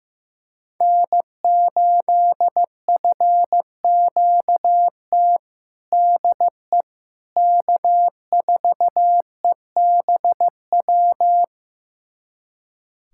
Morse code, heard as a keyed tone, about 15 wpm, 700 Hz. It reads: N8FQT DE K4EBW
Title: USA Callsign DE Callsign - 5 Characters - 15wpm